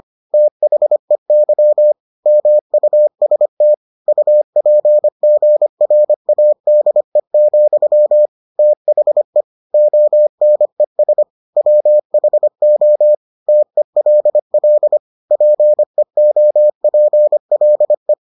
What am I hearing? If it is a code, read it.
Morse code, heard as a keyed tone, about 25 wpm, 600 Hz. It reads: THEY MUST UPGRADE, THE ONES WHO TELL PEOPLE